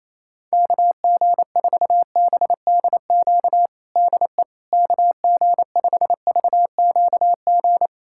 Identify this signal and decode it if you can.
Morse code, heard as a keyed tone, about 28 wpm, 700 Hz. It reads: KG4BDQ DE KG5VQG
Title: USA Callsign DE Callsign 28wpm